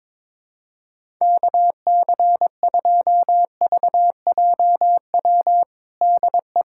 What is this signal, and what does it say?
Morse code, heard as a keyed tone, about 22 wpm, 700 Hz. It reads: KC2VJW DE
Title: USA Callsign DE Callsign 22wpm